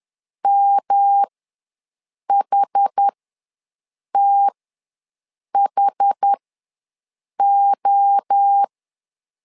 Morse code, heard as a keyed tone, about 10 wpm, spaced as for 6 wpm, 800 Hz. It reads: MHTHO
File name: LEZIONI / 3 MWRO / 3 MWRO ws 10wpm 2310